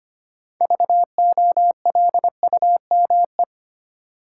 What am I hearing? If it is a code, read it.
Morse code, heard as a keyed tone, about 25 wpm, 700 Hz. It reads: VOLUME
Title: Six Letter Words - 25wpm